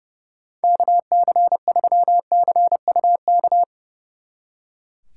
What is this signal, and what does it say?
Morse code, heard as a keyed tone, about 39 wpm, 700 Hz. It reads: KC3CUK